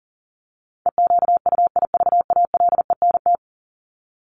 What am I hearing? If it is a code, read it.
Morse code, heard as a keyed tone, about 40 wpm, 700 Hz. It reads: EQUIVALENT
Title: Top 1501-1750 Words 40wpm